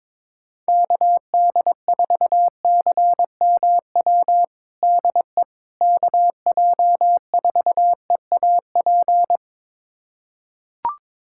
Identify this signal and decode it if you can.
Morse code, heard as a keyed tone, about 22 wpm, 700 Hz. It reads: KD4CMW DE KJ4EAP
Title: USA Callsign DE Callsign 22wpm